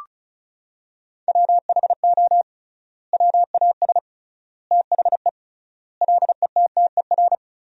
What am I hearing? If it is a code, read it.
Morse code, heard as a keyed tone, about 36 wpm, 700 Hz. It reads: WHO WAS THE LETTER
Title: 3x Word-Spacing - Sentences from Top 300 Words - 35wpm